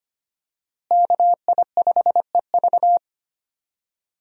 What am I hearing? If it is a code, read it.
Morse code, heard as a keyed tone, about 25 wpm, 700 Hz. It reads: KI5EV
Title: USA Call Signs - 5 Characters - Repeated 3x - 25wpm